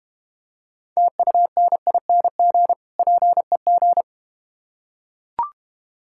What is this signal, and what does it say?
Morse code, heard as a keyed tone, about 32 wpm, 700 Hz. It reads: TUNING PEG